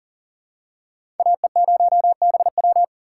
Morse code, heard as a keyed tone, about 40 wpm, 700 Hz. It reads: AE0BW
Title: USA Call Signs - 2x Character Spacing - 40wpm